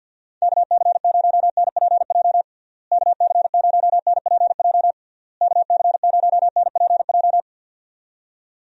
Morse code, heard as a keyed tone, about 50 wpm, 700 Hz. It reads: KC0NPJ KC0NPJ KC0NPJ
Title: USA Call Signs - 6 Characters - Repeated 3x - 50wpm